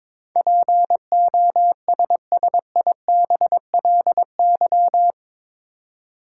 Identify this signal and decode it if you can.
Morse code, heard as a keyed tone, about 22 wpm, 700 Hz. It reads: POSSIBLY
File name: practice / 22wpm / Eight Letter Words - 22wpm